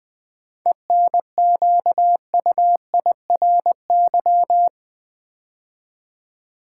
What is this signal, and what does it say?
Morse code, heard as a keyed tone, about 20 wpm, 690 Hz. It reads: ENQUIRY
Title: 7 Letter Words - First 3 Letters Given - 20wpm